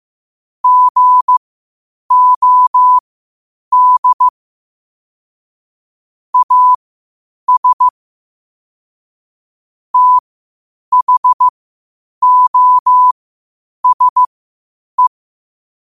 Morse code, heard as a keyed tone, about 15 wpm, spaced as for 8 wpm, 1000 Hz. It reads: GOD AS THOSE
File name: qotdfiles / 8wpm